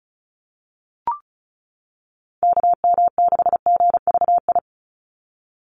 Morse code, heard as a keyed tone, about 34 wpm, 700 Hz. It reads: KM6GVI
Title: USA Call Signs 35wpm